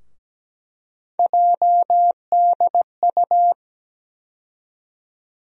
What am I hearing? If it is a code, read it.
Morse code, heard as a keyed tone, about 17 wpm, 700 Hz. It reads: JDU
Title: USA Call Sign Postfixes - 3 Letters 17wpm